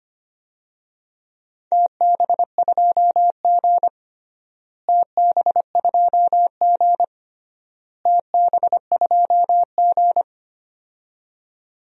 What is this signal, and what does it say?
Morse code, heard as a keyed tone, about 25 wpm, 700 Hz. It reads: TB2G TB2G TB2G
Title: DXCC-100 Call Signs - 4 Characters - Repeated 3x - 25wpm